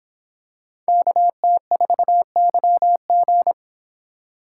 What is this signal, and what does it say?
Morse code, heard as a keyed tone, about 25 wpm, 700 Hz. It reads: KT4YG